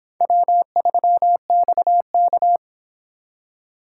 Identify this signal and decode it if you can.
Morse code, heard as a keyed tone, about 24 wpm, 700 Hz. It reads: W3XK